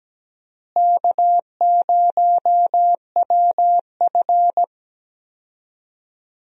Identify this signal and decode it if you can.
Morse code, heard as a keyed tone, about 17 wpm, 700 Hz. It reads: K0WF
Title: USA Call Signs - 4 Characters - Repeated 3x - 17wpm